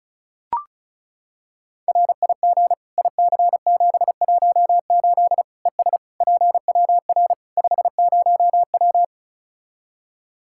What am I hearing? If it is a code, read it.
Morse code, heard as a keyed tone, about 36 wpm, 700 Hz. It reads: RIG IC718 ES PWR 50W
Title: Practice QSO - Exchange 2 - Info - 35wpm